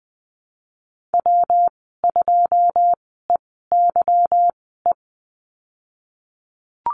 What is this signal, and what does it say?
Morse code, heard as a keyed tone, about 20 wpm, spaced as for 15 wpm, 700 Hz. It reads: W2EYE
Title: 154 Lesson 25g - Letter p - Call Signs 20wpm